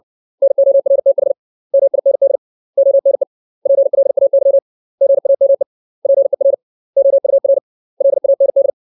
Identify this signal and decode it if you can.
Morse code, heard as a keyed tone, about 60 wpm, 550 Hz. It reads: NORTH MIND ONE PLAY GAME PER OUR LAND
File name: 632 words / 632 Words 60WPM